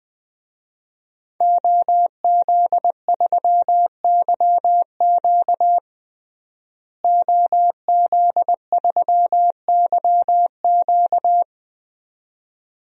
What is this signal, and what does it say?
Morse code, heard as a keyed tone, about 21 wpm, 700 Hz. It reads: OZ3YQ OZ3YQ
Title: DXCC-100 Call Signs - 5 Characters - Repeated 2x - 20wpm